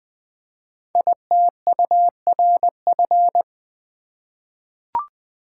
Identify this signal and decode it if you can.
Morse code, heard as a keyed tone, about 20 wpm, 700 Hz. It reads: ITURF
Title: Code Groups - 5 Letters - 20wpm